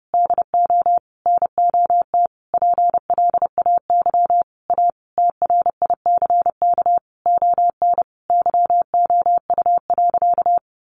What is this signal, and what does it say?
Morse code, heard as a keyed tone, about 30 wpm, 700 Hz. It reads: DO NOT PLAY A TRICK ON YOU.
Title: Sentences from Top 1500 Words 30wpm